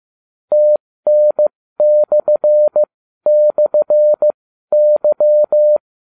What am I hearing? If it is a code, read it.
Morse code, heard as a keyed tone, about 15 wpm, spaced as for 12 wpm, 600 Hz. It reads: TN//Y